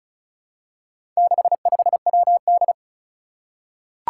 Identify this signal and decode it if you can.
Morse code, heard as a keyed tone, about 36 wpm, 700 Hz. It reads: 65WD
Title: Four Challenging Letter-Number Combinations 35wpm